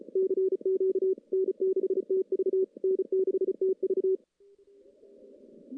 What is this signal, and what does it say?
Morse code, heard as a keyed tone, about 34 wpm, 380 Hz. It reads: CQ N6TV N6TV
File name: contests / CW-SS-k5tr- -1600